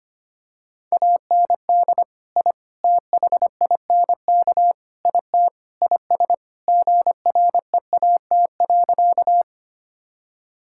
Morse code, heard as a keyed tone, about 25 wpm, 700 Hz. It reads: AND I THINK IT IS GREAT.
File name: practice / 25wpm / Sentences with 4-Word Phrases - 25wpm